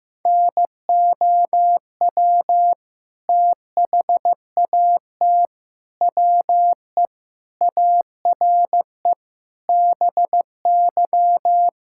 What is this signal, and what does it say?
Morse code, heard as a keyed tone, about 16 wpm, 700 Hz. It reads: NOW THAT WE ARE BY
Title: Sentences from Top 1600 Words 15wpm